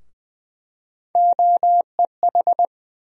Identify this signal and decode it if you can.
Morse code, heard as a keyed tone, about 20 wpm, 700 Hz. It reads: OEH
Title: USA Call Sign Postfixes - 3 Letters 20wpm